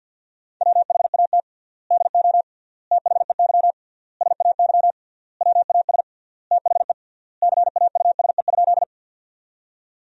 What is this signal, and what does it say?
Morse code, heard as a keyed tone, about 50 wpm, 700 Hz. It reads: WHAT DO THEY SAY WAS THE CAUSE?
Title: 3x Word-Spacing - Sentences from Top 200 Words - 50wpm